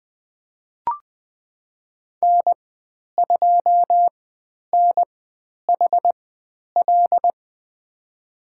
Morse code, heard as a keyed tone, about 20 wpm, spaced as for 10 wpm, 700 Hz. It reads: N2NHL
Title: 095 Lesson 16g - Letter c - Call Signs 20wpm